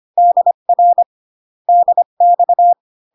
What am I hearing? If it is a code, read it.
Morse code, heard as a keyed tone, about 25 wpm, 700 Hz. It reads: DR DX